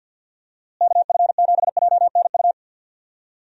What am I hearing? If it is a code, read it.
Morse code, heard as a keyed tone, about 51 wpm, 700 Hz. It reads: KF7JNU
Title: USA Call Signs - 6 Characters - Repeated 2x - 50wpm